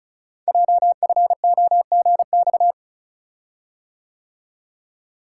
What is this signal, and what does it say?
Morse code, heard as a keyed tone, about 35 wpm, 700 Hz. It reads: JFOGX
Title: Code Groups - 5 Letters - 35wpm